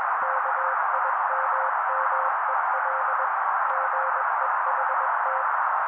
Morse code, heard as a keyed tone, about 21 wpm, 530 Hz. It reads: KIMMEL GEHT